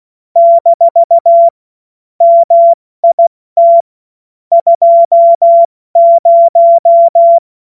Morse code, heard as a keyed tone, about 16 wpm, spaced as for 14 wpm, 680 Hz. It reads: - MIT 20
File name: Morsekurs / Witze / Witz 29. Juni 2021, 70 F80